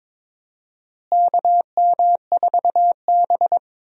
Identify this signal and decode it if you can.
Morse code, heard as a keyed tone, about 22 wpm, 700 Hz. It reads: KM4B
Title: USA Call Signs - 4 Characters - Repeated 3x - 22wpm